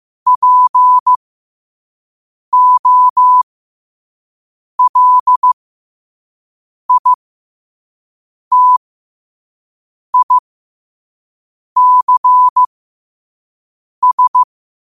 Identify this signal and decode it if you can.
Morse code, heard as a keyed tone, about 15 wpm, spaced as for 5 wpm, 1000 Hz. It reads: POLITICS
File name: qotdfiles / 5wpm